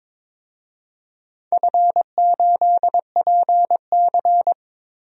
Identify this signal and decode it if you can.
Morse code, heard as a keyed tone, about 22 wpm, 700 Hz. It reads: F8PC
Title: DXCC-100 Call Signs - 4 Characters - Repeated 3x - 22wpm